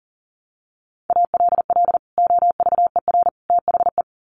Morse code, heard as a keyed tone, about 40 wpm, 700 Hz. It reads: ALL OVER THE
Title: Sets of 3 Words 40wpm